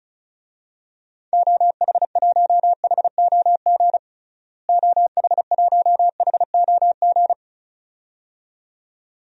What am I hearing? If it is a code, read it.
Morse code, heard as a keyed tone, about 34 wpm, 700 Hz. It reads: OH1HOG OH1HOG
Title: DXCC-100 Call Signs - 6 Characters - Repeated 2x - 35wpm